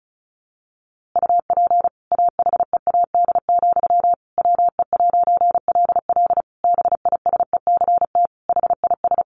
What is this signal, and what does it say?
Morse code, heard as a keyed tone, about 34 wpm, 700 Hz. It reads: UP AHEAD, WE'LL BISECT HIS
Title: A Sound Of Thunder 35wpm